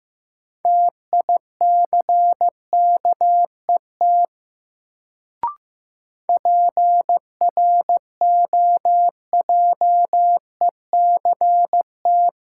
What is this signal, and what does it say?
Morse code, heard as a keyed tone, about 15 wpm, 700 Hz. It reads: TICKET PROJECT
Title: Sentences from Top 1400 Words - New Word Drill - 15wpm